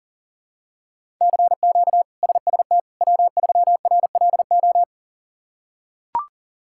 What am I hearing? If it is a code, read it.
Morse code, heard as a keyed tone, about 37 wpm, 690 Hz. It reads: CQ SST W3RLO
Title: K1USN SST QSO Practice - 40wpm